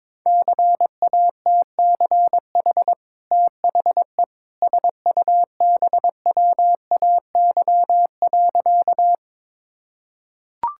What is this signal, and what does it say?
Morse code, heard as a keyed tone, about 22 wpm, 700 Hz. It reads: CATCH THE SUBWAY.
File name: practice / 22wpm / Easy Sentences with Hints - 22wpm